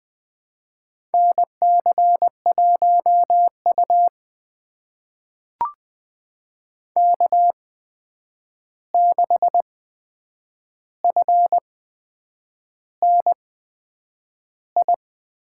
Morse code, heard as a keyed tone, about 20 wpm, spaced as for 8 wpm, 700 Hz. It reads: NC1U K6FNI
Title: USA Call Signs - 8x Character Spacing - 20wpm